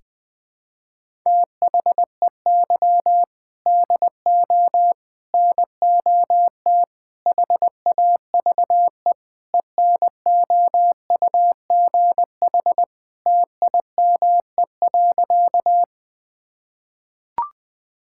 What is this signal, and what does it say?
Morse code, heard as a keyed tone, about 20 wpm, 700 Hz. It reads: THEY DO NOT HAVE ENOUGH TIME.
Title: Sentences with 4-Word Phrases - Spoken Beginning - 20wpm